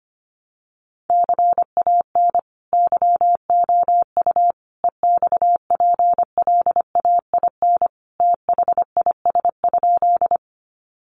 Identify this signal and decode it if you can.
Morse code, heard as a keyed tone, about 25 wpm, 700 Hz. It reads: CAN YOU EXPLAIN THIS?
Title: Sentences from Top 1100 Words 25wpm